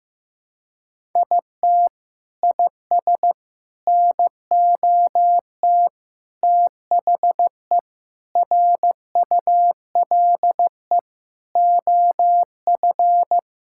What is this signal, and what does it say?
Morse code, heard as a keyed tone, about 16 wpm, 700 Hz. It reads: IT IS NOT THE RULE OF
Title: Sentences from Top 1000 Words - Review Part 5 - 15wpm